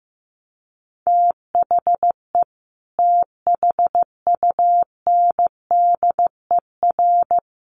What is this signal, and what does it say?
Morse code, heard as a keyed tone, about 15 wpm, 700 Hz. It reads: THE THUNDER
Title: A Sound Of Thunder 15wpm